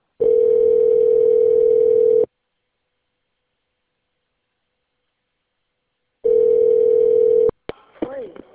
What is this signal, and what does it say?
Ringback tone (North America), 440 + 480 Hz, with speech.
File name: out- - - - - .112